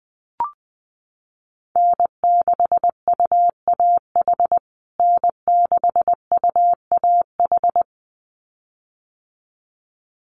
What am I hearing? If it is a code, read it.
Morse code, heard as a keyed tone, about 20 wpm, 700 Hz. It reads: N6UAH N6UAH
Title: USA Call Signs Repeated 2x 20wpm